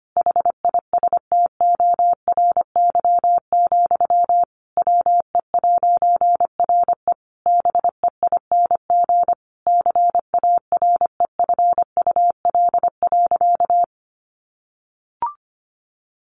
Morse code, heard as a keyed tone, about 25 wpm, 700 Hz. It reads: HISTORY, WE'RE BEING CAREFUL.
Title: A Sound Of Thunder 25wpm